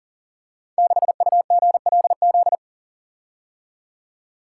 Morse code, heard as a keyed tone, about 42 wpm, 700 Hz. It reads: 6UGLZ
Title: Code Groups - 5 Letter:Numbers - 40wpm